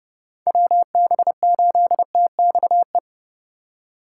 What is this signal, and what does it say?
Morse code, heard as a keyed tone, about 37 wpm, 700 Hz. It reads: WB8TXE
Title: Speed Racing - USA 2x3 Call Signs 30wpm